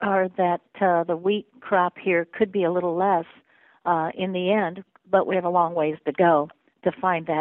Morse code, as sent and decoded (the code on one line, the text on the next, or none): none